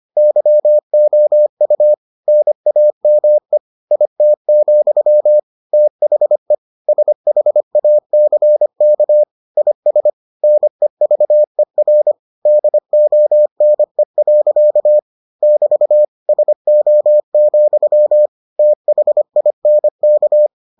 Morse code, heard as a keyed tone, about 25 wpm, 600 Hz. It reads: YOU NAME IT, THE SHACK IS NEVER DONE. <BT> SO, THINK